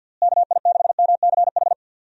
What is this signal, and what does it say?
Morse code, heard as a keyed tone, about 49 wpm, 700 Hz. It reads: KI6MCH
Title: USA Call Signs - 6 Characters - Repeated 2x - 50wpm